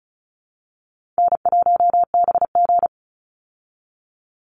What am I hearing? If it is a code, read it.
Morse code, heard as a keyed tone, about 34 wpm, 700 Hz. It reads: N1BG
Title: Easy Call Signs Repeated 3x - 35wpm